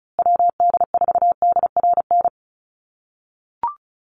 Morse code, heard as a keyed tone, about 34 wpm, 700 Hz. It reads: WD4DRN
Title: USA Call Signs Repeated 2x 35wpm